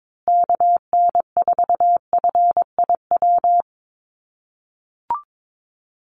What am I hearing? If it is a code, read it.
Morse code, heard as a keyed tone, about 22 wpm, 700 Hz. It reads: KN4FIW
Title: USA Call Signs Repeated 2x 22wpm